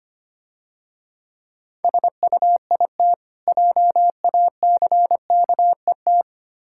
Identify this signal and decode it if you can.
Morse code, heard as a keyed tone, about 25 wpm, 700 Hz. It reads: SUIT JACKET
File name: practice / 25wpm / Difficult Words - Speed-Racing with Hints - 25wpm